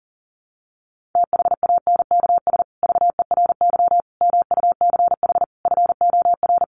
Morse code, heard as a keyed tone, about 40 wpm, 700 Hz. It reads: THANKS VERY MUCH FOR
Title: Sets of 5 Words Encore2 40wpm